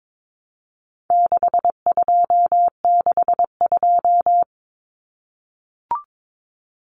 Morse code, heard as a keyed tone, about 22 wpm, 700 Hz. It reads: 6262
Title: Numbers - 4-Digit 22wpm